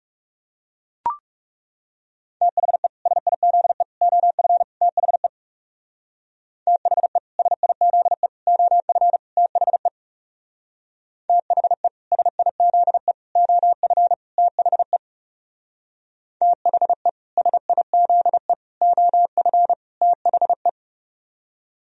Morse code, heard as a keyed tone, about 35 wpm, 700 Hz. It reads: THE SIZE OF THE THE SIZE OF THE THE SIZE OF THE THE SIZE OF THE
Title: Speed-Racing - Sets of 4 Words 30wpm